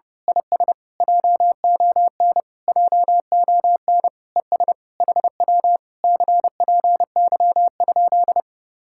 Morse code, heard as a keyed tone, about 28 wpm, 700 Hz. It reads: IS JON JON ES HW CPY?
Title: Speed-Racing Common QSO Phrases - 30wpm